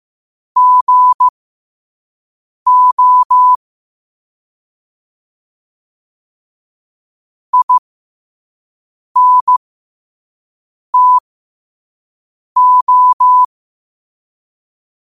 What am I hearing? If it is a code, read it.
Morse code, heard as a keyed tone, about 15 wpm, spaced as for 5 wpm, 1000 Hz. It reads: GO INTO
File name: qotdfiles / 5wpm